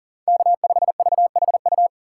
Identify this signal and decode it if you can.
Morse code, heard as a keyed tone, about 38 wpm, 700 Hz. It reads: K5VHU
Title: USA Call Signs - 2x Character Spacing - 40wpm